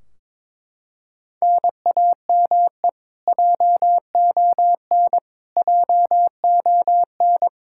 Morse code, heard as a keyed tone, about 22 wpm, 700 Hz. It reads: NAME JON JON
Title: K1USN SST QSO Practice 22wpm